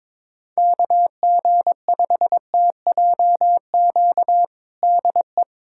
Morse code, heard as a keyed tone, about 22 wpm, 700 Hz. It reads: KG5TJQ DE
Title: USA Callsign DE Callsign - 6 Characters - 22wpm